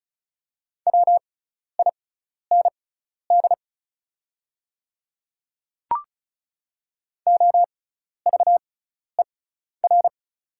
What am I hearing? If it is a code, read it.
Morse code, heard as a keyed tone, about 36 wpm, spaced as for 12 wpm, 700 Hz. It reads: WIND OVER
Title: 6x Character Spacing - Top 500 Words - 35wpm